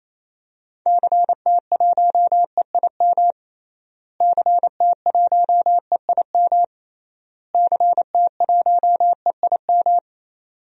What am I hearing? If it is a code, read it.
Morse code, heard as a keyed tone, about 27 wpm, 700 Hz. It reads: CT1EIM CT1EIM CT1EIM
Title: DXCC-100 Call Signs - 6 Characters - Repeated 3x - 28wpm